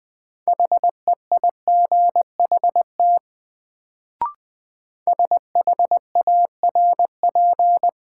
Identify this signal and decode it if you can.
Morse code, heard as a keyed tone, about 21 wpm, 700 Hz. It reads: HEIGHT SHARP
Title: Sentences from Top 1600 Words - New Word Drill - 20wpm